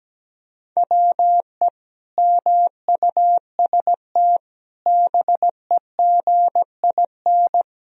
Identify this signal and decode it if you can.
Morse code, heard as a keyed tone, about 17 wpm, 700 Hz. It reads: WE MUST BEGIN
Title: Sentences from Top 300 Words and Abbreviations 17wpm